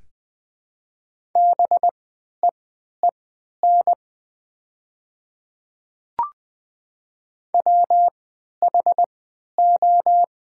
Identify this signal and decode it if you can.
Morse code, heard as a keyed tone, about 21 wpm, spaced as for 11 wpm, 700 Hz. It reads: BEEN WHO
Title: 3x Character Spacing - Top 100 Words - 20wpm